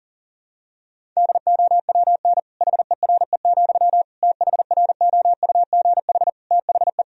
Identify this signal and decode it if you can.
Morse code, heard as a keyed tone, about 40 wpm, 700 Hz. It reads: DOWN HERE, THROUGH THE
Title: Sentences from Top 1000 Words - Review Part 1 - 40wpm